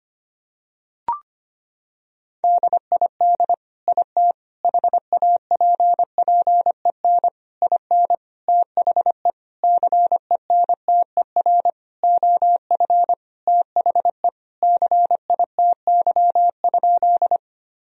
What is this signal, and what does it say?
Morse code, heard as a keyed tone, about 25 wpm, 700 Hz. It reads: DID IT HAPPEN IN THE CENTER OF THE CITY?
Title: Sentences from Top 1000 Words - Review Part 2 - 25wpm